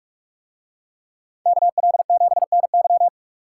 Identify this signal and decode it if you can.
Morse code, heard as a keyed tone, about 44 wpm, 700 Hz. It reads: KL7NY